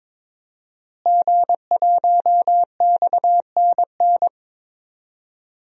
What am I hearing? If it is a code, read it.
Morse code, heard as a keyed tone, about 22 wpm, 700 Hz. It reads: G1XNN
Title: DXCC-100 Call Signs - 5 Characters - Repeated 2x - 22wpm